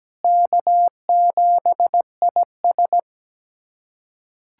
Morse code, heard as a keyed tone, about 17 wpm, 700 Hz. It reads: K7IS
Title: USA Callsign DE Callsign - 4 Characters - 17wpm